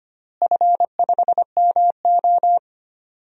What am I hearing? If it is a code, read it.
Morse code, heard as a keyed tone, about 25 wpm, 700 Hz. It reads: F5MO
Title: DXCC-100 Call Signs - 4 Characters - Repeated 2x - 25wpm